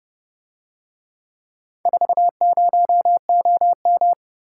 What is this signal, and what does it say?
Morse code, heard as a keyed tone, about 30 wpm, 700 Hz. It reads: 40OM